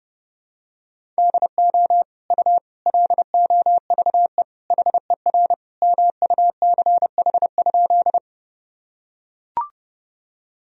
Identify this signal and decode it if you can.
Morse code, heard as a keyed tone, about 30 wpm, 700 Hz. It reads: DO U LOVE HER MUCH?
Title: Sentences from Top 400 Words and Abbreviations 30wpm